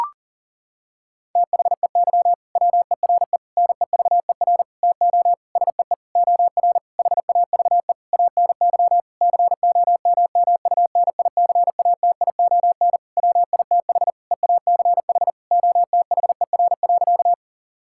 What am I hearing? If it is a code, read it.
Morse code, heard as a keyed tone, about 40 wpm, 700 Hz. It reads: THEY WERE NEVER TO SEE OR HAVE ANY COMMUNICATION WITH EACH OTHER.
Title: Sentences from Top 1000 Words - Review Part 6 - 40wpm